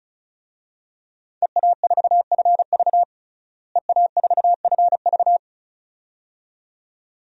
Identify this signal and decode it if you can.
Morse code, heard as a keyed tone, about 36 wpm, 700 Hz. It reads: EA4FV EA4FV
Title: DXCC-100 Call Signs - 5 Characters - Repeated 2x - 35wpm